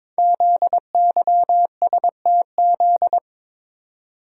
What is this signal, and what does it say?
Morse code, heard as a keyed tone, about 22 wpm, 700 Hz. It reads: ZYSTZ
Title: Code Groups - 5 Letters - 22wpm